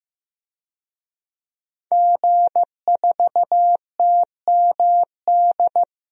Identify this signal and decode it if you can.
Morse code, heard as a keyed tone, about 15 wpm, 700 Hz. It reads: G4TMD
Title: DXCC-100 Call Signs - 5 Characters - 15wpm